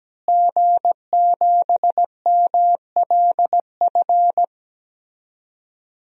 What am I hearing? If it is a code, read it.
Morse code, heard as a keyed tone, about 17 wpm, 700 Hz. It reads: G7MLF
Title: DXCC-100 Call Signs - 5 Characters - Repeated 3x - 17wpm